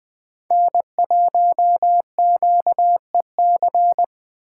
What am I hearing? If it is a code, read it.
Morse code, heard as a keyed tone, about 20 wpm, 700 Hz. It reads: N1QEC
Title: Speed Racing - USA Call Signs 17wpm